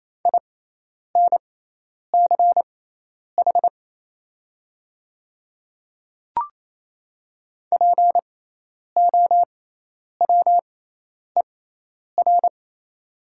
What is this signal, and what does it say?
Morse code, heard as a keyed tone, about 28 wpm, spaced as for 10 wpm, 690 Hz. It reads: INCH POWER
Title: 6x Character Spacing - Top 500 Words - 28wpm